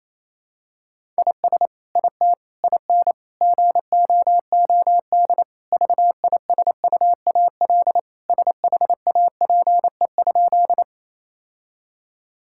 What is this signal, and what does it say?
Morse code, heard as a keyed tone, about 28 wpm, 700 Hz. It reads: IS IT IN GOOD VISUAL SHAPE?
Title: Sentences from Top 1000 Words - Review Part 6 - 28wpm